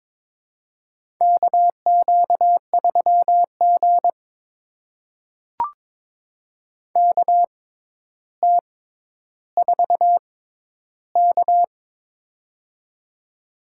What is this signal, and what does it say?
Morse code, heard as a keyed tone, about 22 wpm, spaced as for 11 wpm, 700 Hz. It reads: KQ3G KT4K
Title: USA Call Signs - 4 Characters - 6x Character Spacing - 22wpm